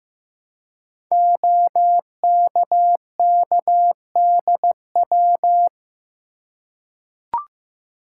Morse code, heard as a keyed tone, about 15 wpm, 700 Hz. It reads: OKKDW
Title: Code Groups - 5 Letter:Numbers - 15wpm